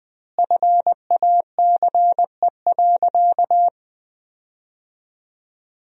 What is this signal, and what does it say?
Morse code, heard as a keyed tone, about 20 wpm, 700 Hz. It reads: FACE.
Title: 4x Word-Spacing - Sentences from Top 300 Words - 20wpm